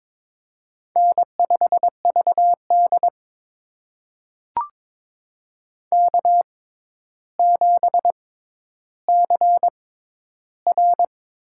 Morse code, heard as a keyed tone, about 22 wpm, spaced as for 11 wpm, 700 Hz. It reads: N5VD K7CR
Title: USA Call Signs - 4 Characters - 6x Character Spacing - 22wpm